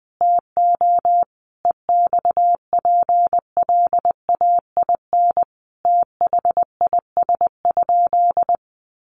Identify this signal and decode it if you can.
Morse code, heard as a keyed tone, about 20 wpm, 700 Hz. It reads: TO EXPLAIN THIS?
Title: Sentences from Top 1100 Words 20wpm